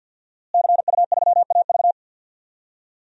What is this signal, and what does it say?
Morse code, heard as a keyed tone, about 53 wpm, 700 Hz. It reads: CU3AV
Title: DXCC-100 Call Signs - 5 Characters - Repeated 2x - 50wpm